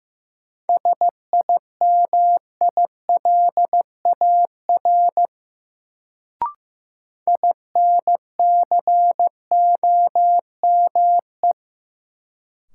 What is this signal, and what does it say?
Morse code, heard as a keyed tone, about 15 wpm, 700 Hz. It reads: SIMILAR INCOME
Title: Sentences from Top 1000 Words - Review Part 5 - New Word Drill - 15wpm